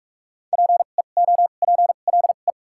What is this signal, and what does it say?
Morse code, heard as a keyed tone, about 46 wpm, spaced as for 32 wpm, 700 Hz. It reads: PEOPLE
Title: 2x Character Spacing - Top 200 Words - 45wpm